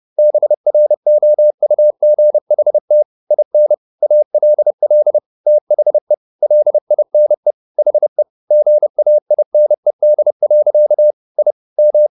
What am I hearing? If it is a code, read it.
Morse code, heard as a keyed tone, about 30 wpm, 600 Hz. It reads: BROUGHT IN ALL THE LINE HE GAINED. I M